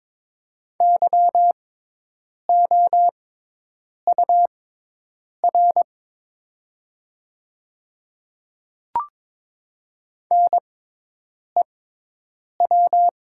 Morse code, heard as a keyed tone, about 22 wpm, spaced as for 8 wpm, 700 Hz. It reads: YOUR NEW
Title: 6x Character Spacing - Top 300 Words - 22wpm